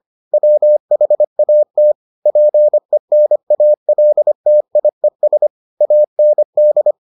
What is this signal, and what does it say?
Morse code, heard as a keyed tone, about 25 wpm, 600 Hz. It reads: WHAT PENALTIES AND